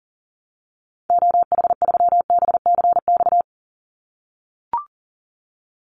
Morse code, heard as a keyed tone, about 40 wpm, 700 Hz. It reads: OH3BCX
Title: Call Signs - 40wpm